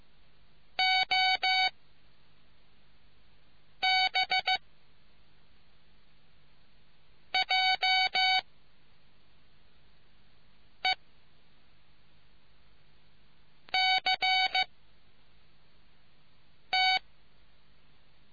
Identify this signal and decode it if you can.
Morse code, heard as a keyed tone, about 15 wpm, spaced as for 3 wpm, 750 Hz. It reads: OBJECT